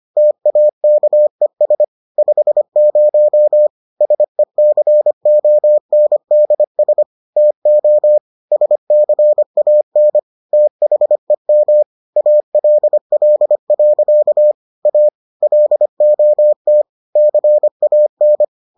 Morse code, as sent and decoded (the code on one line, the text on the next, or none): - .- -.- . ... / ..... ----- / ... -.-. --- -. -.. ... / - --- / ... -.-. .- -. / - .... -- / .- .-.. .-.. .-.-.- / .- / .-.. --- - / -.-. .- -.
TAKES 50 SECONDS TO SCAN THEM ALL. A LOT CAN